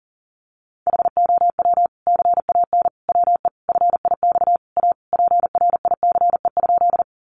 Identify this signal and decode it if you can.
Morse code, heard as a keyed tone, about 40 wpm, 700 Hz. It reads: HOW CAN WE FIX A PRICE?
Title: Sentences from Top 1400 Words 40wpm